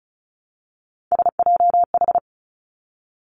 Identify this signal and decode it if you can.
Morse code, heard as a keyed tone, about 36 wpm, 700 Hz. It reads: SJH
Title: Three Challenging Letter-Number Combinations 35wpm